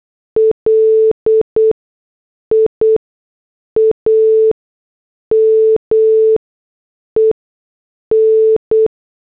Morse code, heard as a keyed tone, about 8 wpm, spaced as for 6 wpm, 440 Hz. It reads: LIAMEN
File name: morse / headlines